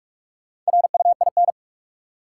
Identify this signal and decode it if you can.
Morse code, heard as a keyed tone, about 46 wpm, 700 Hz. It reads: RUIN
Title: Four Letter Words - 45wpm